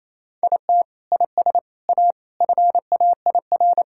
Morse code, heard as a keyed tone, about 28 wpm, 700 Hz. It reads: IT IS A FAIR
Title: Sentences from Top 1600 Words 28wpm